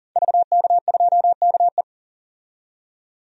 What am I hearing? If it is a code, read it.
Morse code, heard as a keyed tone, about 40 wpm, 700 Hz. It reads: VK2KE
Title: DXCC-100 Call Signs - 5 Characters - 40wpm